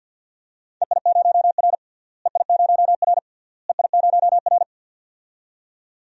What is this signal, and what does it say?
Morse code, heard as a keyed tone, about 50 wpm, 700 Hz. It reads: EI0R EI0R EI0R
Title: DXCC-100 Call Signs - 4 Characters - Repeated 3x - 50wpm